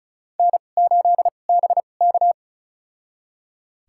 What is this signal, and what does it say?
Morse code, heard as a keyed tone, about 36 wpm, spaced as for 25 wpm, 700 Hz. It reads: N8BK
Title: USA Call Signs - 4 Characters - 2x Character Spacing - 35wpm